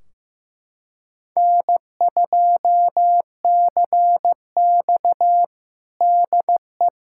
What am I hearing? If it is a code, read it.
Morse code, heard as a keyed tone, about 15 wpm, 700 Hz. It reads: N2CX DE
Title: K1USN SST QSO Practice 15wpm